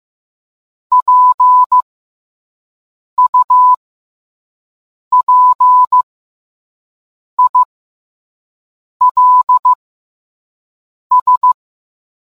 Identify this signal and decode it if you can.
Morse code, heard as a keyed tone, about 15 wpm, spaced as for 5 wpm, 1000 Hz. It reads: PUPILS